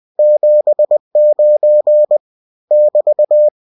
Morse code, heard as a keyed tone, about 20 wpm, 600 Hz. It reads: 79 <BT>
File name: cw-books / winnie-the-pooh / winnie-20wpm-page-0079